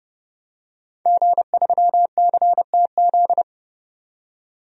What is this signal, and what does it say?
Morse code, heard as a keyed tone, about 29 wpm, 700 Hz. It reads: G3CTZ
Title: DXCC-100 Call Signs - 5 Characters - 30wpm